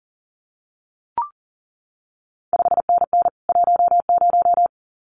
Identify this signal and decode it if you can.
Morse code, heard as a keyed tone, about 38 wpm, 700 Hz. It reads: 5NN 10
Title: CQ WW DX Contest Exchange Encore - 40wpm